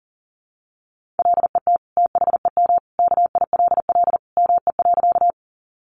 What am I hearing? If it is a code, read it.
Morse code, heard as a keyed tone, about 40 wpm, 700 Hz. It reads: LET THEM KILL ME.
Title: Sentences from Top 1400 Words 40wpm